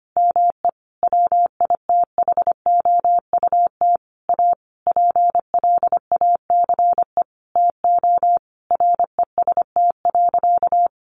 Morse code, heard as a keyed tone, about 25 wpm, 700 Hz. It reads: ME WITHOUT A PLACE TO REST.
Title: Sentences from Top 600 Words 25wpm